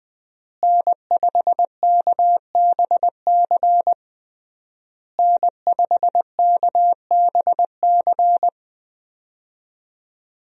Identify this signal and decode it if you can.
Morse code, heard as a keyed tone, about 21 wpm, 700 Hz. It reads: N5KBC N5KBC
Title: USA Call Signs - 5 Characters - Repeated 3x - 20wpm